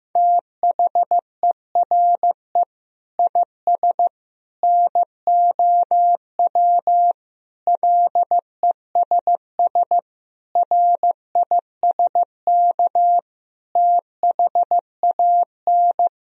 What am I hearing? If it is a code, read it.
Morse code, heard as a keyed tone, about 15 wpm, 700 Hz. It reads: THERE IS NOW LESS RISK THAN
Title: Sentences from Top 1000 Words - Review Part 4 - 15wpm